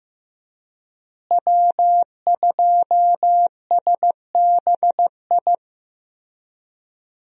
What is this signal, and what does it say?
Morse code, heard as a keyed tone, about 15 wpm, 700 Hz. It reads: W2SBI
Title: USA Call Signs - 5 Characters - Repeated 3x - 15wpm